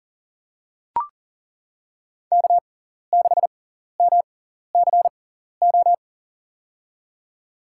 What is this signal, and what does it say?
Morse code, heard as a keyed tone, about 40 wpm, spaced as for 14 wpm, 700 Hz. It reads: K6MCO
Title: USA Call Signs - 5 Characters - 6x Character Spacing - 40wpm